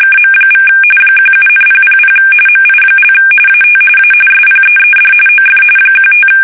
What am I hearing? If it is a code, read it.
RTTY, about 45.45 baud. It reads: KM= 17,000 MI) AT A SPEED OF 15
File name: fsk850 / USS-Massachusetts